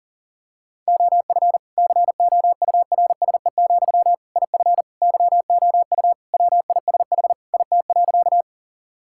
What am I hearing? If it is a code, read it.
Morse code, heard as a keyed tone, about 40 wpm, 700 Hz. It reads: OF COURSE, IF YOU WISH IT.
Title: Sentences from Top 1600 Words 40wpm